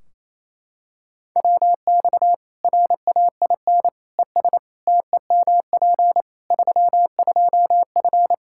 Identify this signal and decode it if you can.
Morse code, heard as a keyed tone, about 27 wpm, 700 Hz. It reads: WX RAIN ES TEMP 32F
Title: K1USN SST QSO Practice 28wpm